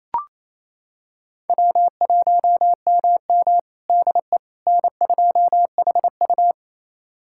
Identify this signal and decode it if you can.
Morse code, heard as a keyed tone, about 28 wpm, 700 Hz. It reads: W1MM DE N2HU
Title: USA Callsign DE Callsign - 4 Characters - 28wpm